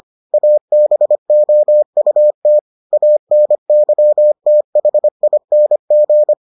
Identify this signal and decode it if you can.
Morse code, heard as a keyed tone, about 25 wpm, 600 Hz. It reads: ABOUT ANYTHING